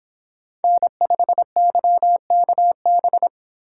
Morse code, heard as a keyed tone, about 25 wpm, 700 Hz. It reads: N5YKB